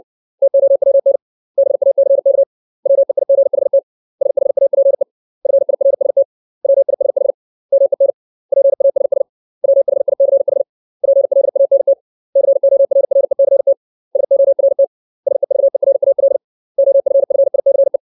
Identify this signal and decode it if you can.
Morse code, heard as a keyed tone, about 60 wpm, 550 Hz. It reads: TOWN BACK WEIGHT SHAPE FIRST WISH MEN PASS WHICH PLANT CORRECT SORT SEVERAL OFFICE
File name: 632 words / 632 Words 60WPM